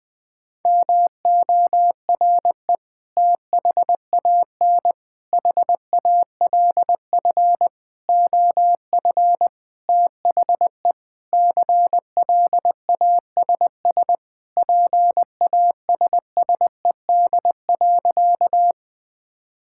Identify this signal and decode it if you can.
Morse code, heard as a keyed tone, about 20 wpm, 700 Hz. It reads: MORE THAN HALF OF THE CLASS PASSED.
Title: Sentences with 5-Word Phrases - Spoken Beginning - 20wpm